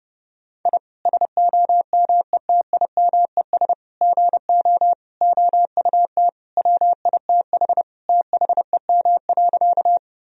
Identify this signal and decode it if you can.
Morse code, heard as a keyed tone, about 30 wpm, 700 Hz. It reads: I SOMETIMES GO OUT WITH THEM.
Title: Sentences from Top 1000 Words - Review Part 3 - 30wpm